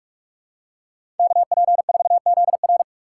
Morse code, heard as a keyed tone, about 42 wpm, 700 Hz. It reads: KP4ZR
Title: USA Call Signs - 5 Characters - 6x Character Spacing - 45wpm